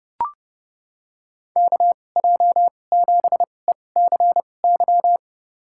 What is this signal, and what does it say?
Morse code, heard as a keyed tone, about 30 wpm, spaced as for 22 wpm, 700 Hz. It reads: KJ7ECY
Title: USA Call Signs - 2x Character Spacing - 30wpm